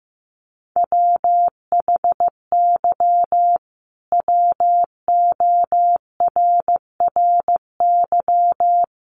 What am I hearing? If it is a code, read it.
Morse code, heard as a keyed tone, about 15 wpm, 700 Hz. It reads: WHY WORRY
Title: Sentences from Top 1100 Words 15wpm